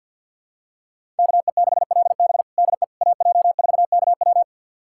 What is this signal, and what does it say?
Morse code, heard as a keyed tone, about 52 wpm, 700 Hz. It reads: KE6LB DE AJ4KW
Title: USA Callsign DE Callsign - 5 Characters - 50wpm